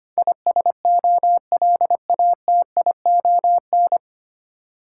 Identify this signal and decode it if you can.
Morse code, heard as a keyed tone, about 25 wpm, 690 Hz. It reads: ISOLATION
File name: practice / 25wpm / 9 Letter Words - First 3 Letters Given - 25wpm